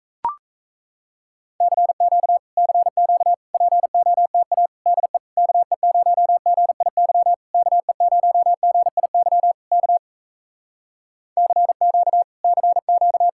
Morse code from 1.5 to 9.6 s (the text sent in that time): CQ CQ POTA DE KE0GIY KE0GIY